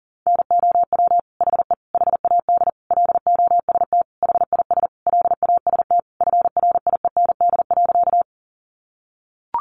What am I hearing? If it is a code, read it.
Morse code, heard as a keyed tone, about 40 wpm, 700 Hz. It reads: NOW HE HAD LOST HIS LAST FRIEND.
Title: Sentences from Top 1400 Words 40wpm